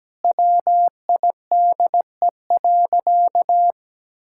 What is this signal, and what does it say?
Morse code, heard as a keyed tone, about 17 wpm, 700 Hz. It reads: WIDE.
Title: Sentences from Top 1000 Words - Review Part 6 - 17wpm